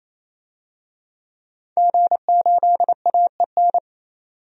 Morse code, heard as a keyed tone, about 28 wpm, 700 Hz. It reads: G8AEN
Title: DXCC-100 Call Signs - 5 Characters - 28wpm